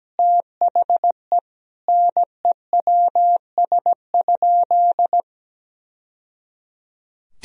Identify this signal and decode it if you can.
Morse code, heard as a keyed tone, about 17 wpm, 700 Hz. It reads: THE NEWS?
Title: Sentences from Top 1000 Words - Review Part 4 - 17wpm